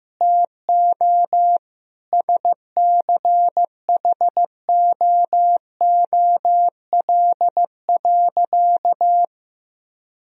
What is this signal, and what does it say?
Morse code, heard as a keyed tone, about 16 wpm, 700 Hz. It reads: TO SCHOOL.
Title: Sentences from Top 1000 Words - Review Part 4 - 15wpm